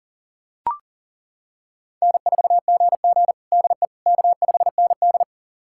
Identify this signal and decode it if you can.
Morse code, heard as a keyed tone, about 40 wpm, 700 Hz. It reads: N4GG DE K5ND
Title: USA Callsign DE Callsign - 4 Characters - 40wpm